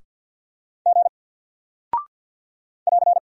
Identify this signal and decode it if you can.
Morse code, heard as a keyed tone, about 48 wpm, 700 Hz. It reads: G <AR>
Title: Single Character (Letter:Number:Punctuation:Prosign) Alt 50wpm